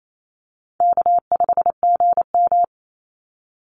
Morse code, heard as a keyed tone, about 28 wpm, 700 Hz. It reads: K5GM
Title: Easy Call Signs - 28wpm